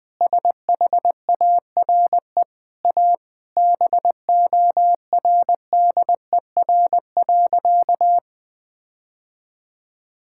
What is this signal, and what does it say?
Morse code, heard as a keyed tone, about 20 wpm, 700 Hz. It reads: SHARE A BORDER.
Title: Sentences with 5-Word Phrases - 20wpm